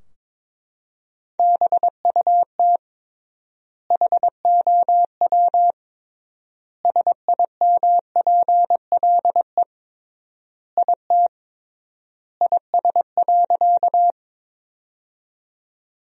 Morse code, heard as a keyed tone, about 22 wpm, 700 Hz. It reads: BUT HOW SIMPLE IT IS.
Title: 3x Word-Spacing - Sentences from Top 400 Words - 22wpm